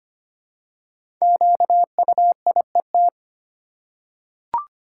Morse code, heard as a keyed tone, about 25 wpm, 700 Hz. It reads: QUIET
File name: 5 Letter Words - First 3 Letters Given - 25wpm